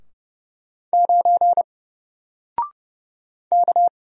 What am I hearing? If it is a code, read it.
Morse code, heard as a keyed tone, about 30 wpm, 700 Hz. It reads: 9 K
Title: Single Character (Letter:Number:Punctuation:Prosign) Alt 30wpm